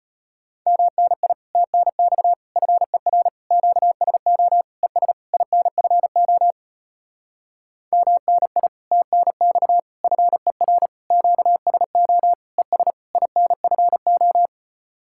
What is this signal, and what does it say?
Morse code, heard as a keyed tone, about 33 wpm, 700 Hz. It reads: MNI TNX FER QSO ES INFO MNI TNX FER QSO ES INFO